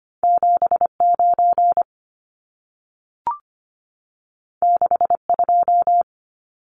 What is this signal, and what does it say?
Morse code, heard as a keyed tone, about 25 wpm, 700 Hz. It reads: 79 62
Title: Numbers - 2-Digit 25wpm